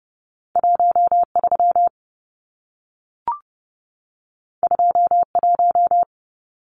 Morse code, heard as a keyed tone, about 30 wpm, 700 Hz. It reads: 13 21
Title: Numbers - 2-Digit 30wpm